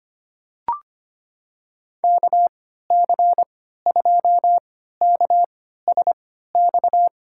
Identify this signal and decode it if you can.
Morse code, heard as a keyed tone, about 25 wpm, spaced as for 14 wpm, 700 Hz. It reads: KC2KSX
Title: USA Call Signs - 3x Character Spacing - 25wpm